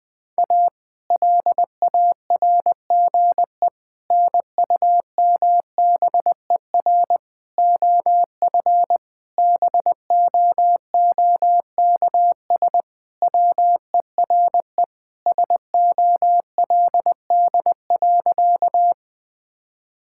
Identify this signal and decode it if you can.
Morse code, heard as a keyed tone, about 21 wpm, 700 Hz. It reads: A LARGE NUMBER OF BOOKS WERE SOLD.
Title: Sentences with 4-Word Phrases - 20wpm